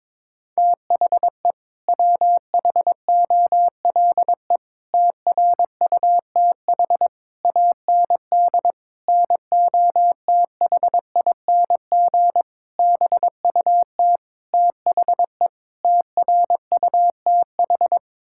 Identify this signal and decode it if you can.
Morse code, heard as a keyed tone, about 22 wpm, 700 Hz. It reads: THE WHOLE TRUTH AND NOTHING BUT THE TRUTH
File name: practice / 22wpm / Sets of 8 Words - 22wpm